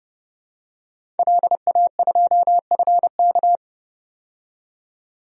Morse code, heard as a keyed tone, about 30 wpm, 700 Hz. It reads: LA2FK
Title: DXCC-100 Call Signs - 5 Characters - 30wpm